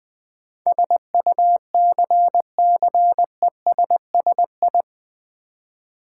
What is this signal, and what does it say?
Morse code, heard as a keyed tone, about 20 wpm, 700 Hz. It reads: SUCCESSI
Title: Build A Word - 4 Letters At A Time - 10-Letter Words - 20wpm